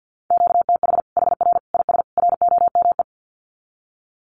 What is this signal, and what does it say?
Morse code, heard as a keyed tone, about 50 wpm, 700 Hz. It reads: QTH HR IS ROME
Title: Common QSO Phrases 50wpm